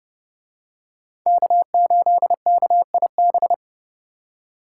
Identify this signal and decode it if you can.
Morse code, heard as a keyed tone, about 29 wpm, 700 Hz. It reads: K8KIB